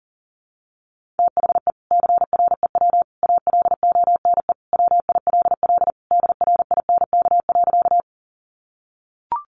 Morse code, heard as a keyed tone, about 40 wpm, 700 Hz. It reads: THE CREW ALONE WILL DRINK.
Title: Sentences from Top 1500 Words 40wpm